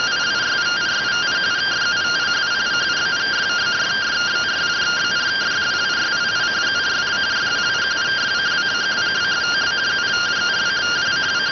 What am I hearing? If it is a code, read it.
Two-tone teleprinter signal, about 170 Hz shift.